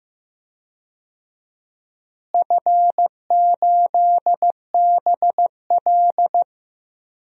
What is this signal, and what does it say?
Morse code, heard as a keyed tone, about 15 wpm, 700 Hz. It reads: F8BL
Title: DXCC-100 Call Signs - 4 Characters - Repeated 2x - 15wpm